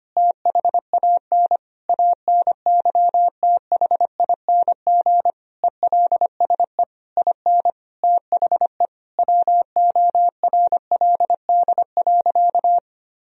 Morse code, heard as a keyed tone, about 25 wpm, 700 Hz. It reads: THAN ANYTHING ELSE IN THE WORLD.
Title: Sentences from Top 1000 Words - Review Part 4 - 25wpm